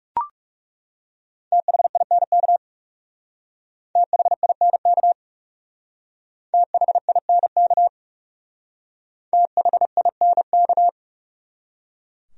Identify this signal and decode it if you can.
Morse code, heard as a keyed tone, about 40 wpm, 700 Hz. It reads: THINK THINK THINK THINK